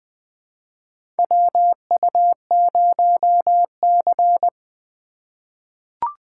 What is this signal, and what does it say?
Morse code, heard as a keyed tone, about 20 wpm, 700 Hz. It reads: WU0C
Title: USA Call Signs - 4 Characters - 2x Character Spacing - 20wpm